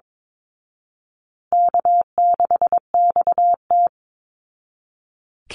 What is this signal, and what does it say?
Morse code, heard as a keyed tone, about 22 wpm, 700 Hz. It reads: K6XT
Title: Easy Call Signs Repeated 3x - 22wpm